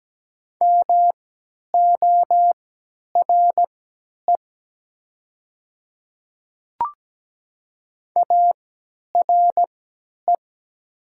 Morse code, heard as a keyed tone, about 17 wpm, spaced as for 10 wpm, 700 Hz. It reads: MORE ARE
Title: 3x Character Spacing - Top 100 Words - 17wpm